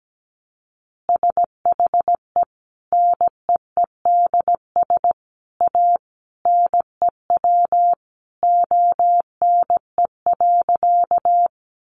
Morse code, heard as a keyed tone, about 17 wpm, 700 Hz. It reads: SHE NEEDS A NEW ONE.
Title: Additional Easy Phrases 17wpm